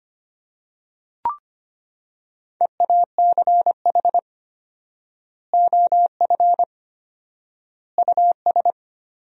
Morse code, heard as a keyed tone, about 25 wpm, 700 Hz. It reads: EACH OF US
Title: 4x Word-Spacing - Sentences from Top 500 Words - 25wpm